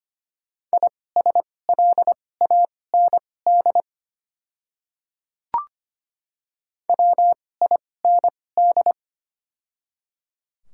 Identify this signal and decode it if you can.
Morse code, heard as a keyed tone, about 25 wpm, spaced as for 18 wpm, 700 Hz. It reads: ISLAND WIND
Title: 2x Character Spacing - Top 500 Words - 25wpm